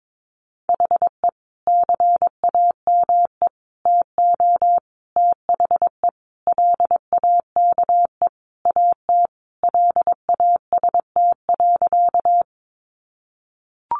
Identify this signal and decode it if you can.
Morse code, heard as a keyed tone, about 22 wpm, 700 Hz. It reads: HE CAME TO THE LAKE AT LAST.
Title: Sentences from Top 1100 Words 22wpm